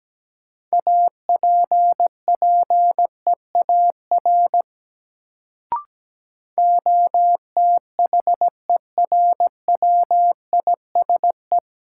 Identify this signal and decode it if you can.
Morse code, heard as a keyed tone, about 17 wpm, 700 Hz. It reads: APPEAR OTHERWISE
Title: Sentences from Top 1200 Words - New Word Drill Alt - 17wpm